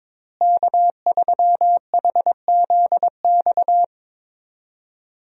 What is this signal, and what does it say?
Morse code, heard as a keyed tone, about 23 wpm, 700 Hz. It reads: K3HZX
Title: USA Call Signs - 5 Characters - Repeated 2x - 22wpm